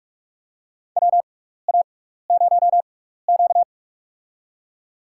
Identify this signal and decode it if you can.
Morse code, heard as a keyed tone, about 44 wpm, spaced as for 15 wpm, 700 Hz. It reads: WA0Q
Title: USA Call Signs - 4 Characters - 6x Character Spacing - 45wpm